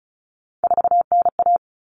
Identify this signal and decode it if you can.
Morse code, heard as a keyed tone, about 34 wpm, 700 Hz. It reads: 4NA
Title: Common Cut Numbers - 3-Digit 35wpm